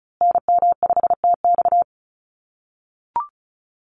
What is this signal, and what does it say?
Morse code, heard as a keyed tone, about 34 wpm, 700 Hz. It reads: NM5TX
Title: USA Call Signs Repeated 2x 35wpm